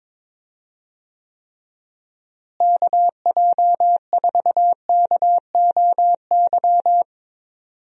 Morse code, heard as a keyed tone, about 22 wpm, 700 Hz. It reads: KJ4KOY
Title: USA Call Signs - 6 Characters - Repeated 3x - 22wpm